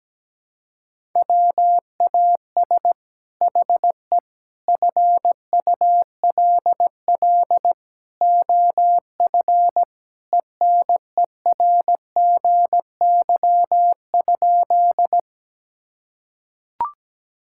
Morse code, heard as a keyed tone, about 17 wpm, 700 Hz. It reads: WAS HE FULL OF ENERGY?
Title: Sentences from Top 1000 Words - Review Part 6 - 17wpm